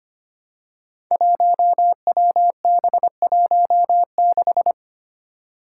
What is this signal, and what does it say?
Morse code, heard as a keyed tone, about 25 wpm, 700 Hz. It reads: 1WB16
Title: Five Challenging Letter-Number Combinations 25wpm